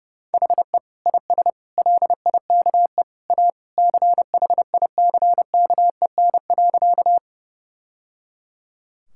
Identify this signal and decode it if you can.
Morse code, heard as a keyed tone, about 30 wpm, 700 Hz. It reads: HE IS LIKE A CHICKEN.
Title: Sentences from Top 1000 Words - Review Part 5 - 30wpm